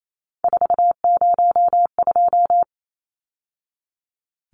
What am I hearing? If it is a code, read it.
Morse code, heard as a keyed tone, about 28 wpm, 700 Hz. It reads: 402
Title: Numbers - 3-Digit 28wpm